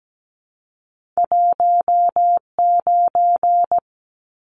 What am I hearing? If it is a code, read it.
Morse code, heard as a keyed tone, about 17 wpm, 700 Hz. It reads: 19
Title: Numbers - 2-Digit 17wpm